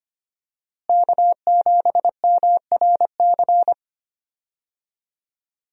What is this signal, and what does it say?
Morse code, heard as a keyed tone, about 25 wpm, 700 Hz. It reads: K7MRC
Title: USA Call Signs - 5 Characters - Repeated 2x - 25wpm